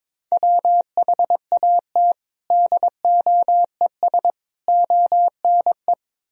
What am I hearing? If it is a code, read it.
Morse code, heard as a keyed tone, about 22 wpm, 700 Hz. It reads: WHAT DOES ONE
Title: Sentences from Top 200 Words and Abbreviations 22wpm